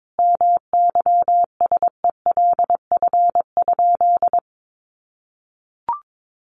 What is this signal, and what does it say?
Morse code, heard as a keyed tone, about 22 wpm, 700 Hz. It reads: MYSELF?
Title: Sentences from Top 1300 Words 22wpm